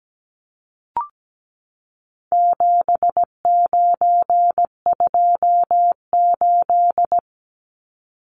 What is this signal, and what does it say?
Morse code, heard as a keyed tone, about 17 wpm, 700 Hz. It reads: 7928
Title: Numbers - 4-Digit 17wpm